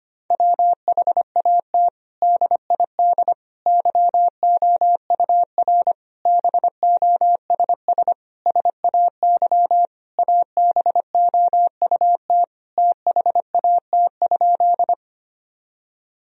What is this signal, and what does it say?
Morse code, heard as a keyed tone, about 25 wpm, 700 Hz. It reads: WHAT DID YOUR BOSS SAY ABOUT THAT?
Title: Sentences from Top 1000 Words - Review Part 4 - 25wpm